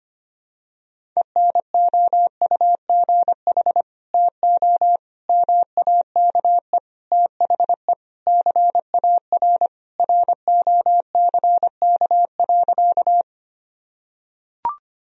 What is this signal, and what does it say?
Morse code, heard as a keyed tone, about 25 wpm, 700 Hz. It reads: ENOUGH TO MAKE THE CAR ROCK.
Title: Sentences from Top 1000 Words - Review Part 1 - 25wpm